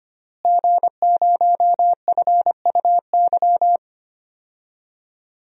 Morse code, heard as a keyed tone, about 25 wpm, 700 Hz. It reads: G0FUY
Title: DXCC-100 Call Signs - 5 Characters - Repeated 2x - 25wpm